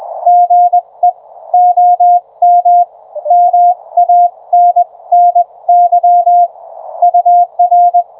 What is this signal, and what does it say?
Morse code, heard as a keyed tone, about 21 wpm, spaced as for 18 wpm, 720 Hz. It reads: GE OM MANNY UR